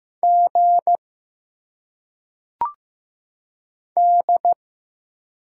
Morse code, heard as a keyed tone, about 15 wpm, 700 Hz. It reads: G D
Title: Single Letters - Phonetic Pronunciation 15wpm